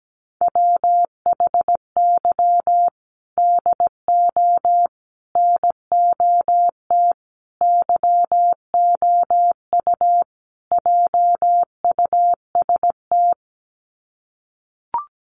Morse code, heard as a keyed tone, about 17 wpm, 700 Hz. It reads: WHY DO NOT YOU JUST
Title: Sets of 5 Words 17wpm